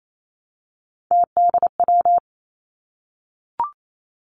Morse code, heard as a keyed tone, about 28 wpm, 700 Hz. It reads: TDW
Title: Three Challenging Letter-Number Combinations 28wpm